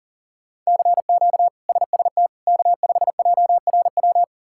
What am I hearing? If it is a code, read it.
Morse code, heard as a keyed tone, about 37 wpm, 700 Hz. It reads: CQ SST K5JRW
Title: K1USN SST QSO Practice - 40wpm